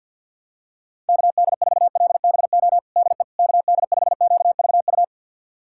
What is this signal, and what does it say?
Morse code, heard as a keyed tone, about 46 wpm, 700 Hz. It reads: KD4LBO DE KD5QVU
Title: USA Callsign DE Callsign - 6 Characters - 50wpm